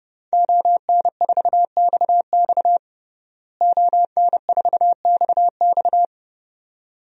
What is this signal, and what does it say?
Morse code, heard as a keyed tone, about 30 wpm, 700 Hz. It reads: ON4XX ON4XX
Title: DXCC-100 Call Signs - 5 Characters - Repeated 2x - 30wpm